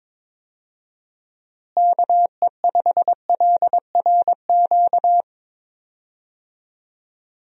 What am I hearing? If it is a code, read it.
Morse code, heard as a keyed tone, about 22 wpm, 700 Hz. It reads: KE5LRQ
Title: USA Call Signs - 6 Characters - Repeated 2x - 22wpm